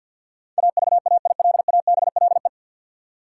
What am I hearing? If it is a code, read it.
Morse code, heard as a keyed tone, about 52 wpm, 700 Hz. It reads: AVAILABLE
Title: Sentences from Top 600 Words - New Word Drill - 50wpm